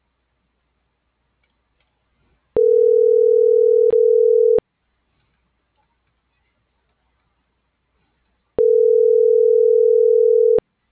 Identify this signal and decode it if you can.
Ringback tone (North America), 440 + 480 Hz.